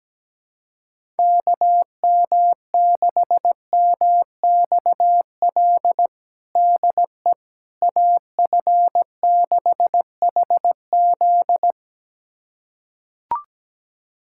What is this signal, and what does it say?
Morse code, heard as a keyed tone, about 17 wpm, 700 Hz. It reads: KM6MXL DE AF6HZ
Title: USA Callsign DE Callsign 17wpm